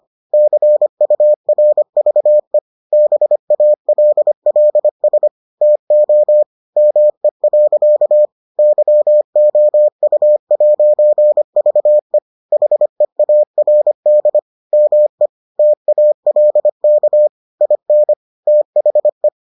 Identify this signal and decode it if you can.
Morse code, heard as a keyed tone, about 25 wpm, 600 Hz. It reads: CURVE BALLS TO ME. YOU'VE HEARD ME TALK IN THE